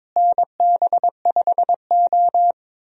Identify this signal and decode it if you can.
Morse code, heard as a keyed tone, about 22 wpm, 700 Hz. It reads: NB5O
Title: USA Call Signs - 4 Characters - Repeated 3x - 22wpm